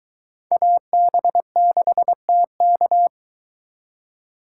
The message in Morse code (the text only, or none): AB6TK